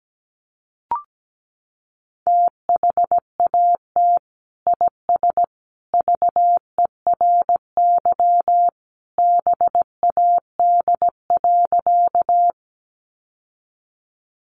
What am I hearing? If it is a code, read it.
Morse code, heard as a keyed tone, about 17 wpm, 700 Hz. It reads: THAT IS VERY BAD.
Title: Sentences from Top 700 Words 17wpm